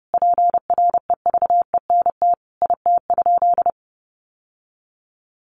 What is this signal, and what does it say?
Morse code, heard as a keyed tone, about 30 wpm, 700 Hz. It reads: PREVENT IT?
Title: Sentences from Top 1100 Words 30wpm